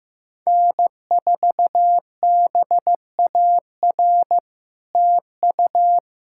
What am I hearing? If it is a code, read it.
Morse code, heard as a keyed tone, about 15 wpm, 700 Hz. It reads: N4BAR TU
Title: SOTA QSO Practice 15wpm